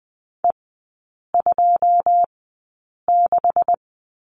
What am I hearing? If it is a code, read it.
Morse code, heard as a keyed tone, about 20 wpm, spaced as for 8 wpm, 700 Hz. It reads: E26
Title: 198 Lesson 32e - Letter k - Three Characters 20wpm